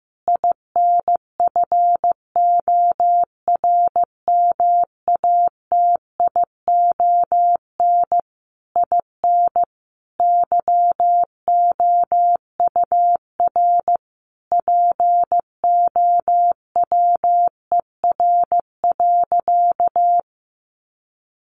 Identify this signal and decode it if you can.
Morse code, heard as a keyed tone, about 15 wpm, 700 Hz. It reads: INFORMATION IN YOUR POWER.
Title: Sentences from Top 600 Words 15wpm